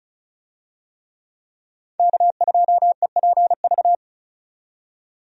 Morse code, heard as a keyed tone, about 36 wpm, 700 Hz. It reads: K2EPV
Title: USA Call Signs - 5 Characters - 35wpm